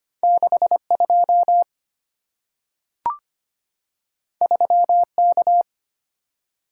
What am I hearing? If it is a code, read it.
Morse code, heard as a keyed tone, about 25 wpm, 700 Hz. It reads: 62 3K
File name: Two Letter-Number Combinations - 25wpm